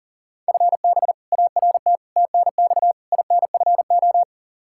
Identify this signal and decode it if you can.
Morse code, heard as a keyed tone, about 40 wpm, 700 Hz. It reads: FB ART TNX INFO
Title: K1USN SST QSO Practice 40wpm